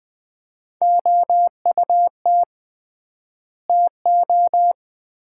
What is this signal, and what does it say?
Morse code, heard as a keyed tone, about 20 wpm, 700 Hz. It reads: OUT TO
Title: 3x Word-Spacing - Sets of 3 Words - 20wpm